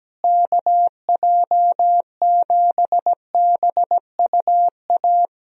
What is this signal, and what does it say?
Morse code, heard as a keyed tone, about 17 wpm, 700 Hz. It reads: KJ7BUA
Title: USA Call Signs - 6 Characters - Repeated 3x - 17wpm